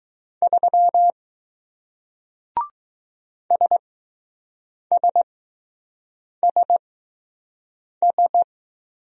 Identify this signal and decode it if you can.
Morse code, heard as a keyed tone, about 20 wpm, spaced as for 6 wpm, 700 Hz. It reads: 3 SSSS